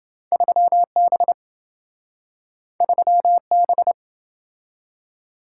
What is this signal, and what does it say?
Morse code, heard as a keyed tone, about 27 wpm, 700 Hz. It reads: 3B 3B